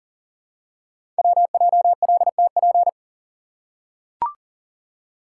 Morse code, heard as a keyed tone, about 41 wpm, 700 Hz. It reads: WJLTP
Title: Code Groups - 5 Letters - 40wpm